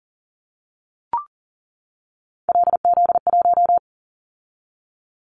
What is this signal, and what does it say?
Morse code, heard as a keyed tone, about 42 wpm, 700 Hz. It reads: LZ1
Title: Callsign Prefixes for Top 50 DXCC - 40wpm